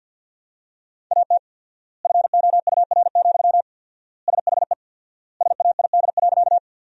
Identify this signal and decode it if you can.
Morse code, heard as a keyed tone, about 48 wpm, 700 Hz. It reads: AT FOUR, SHE SAID.
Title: 4x Word-Spacing - Sentences from Top 300 Words - 50wpm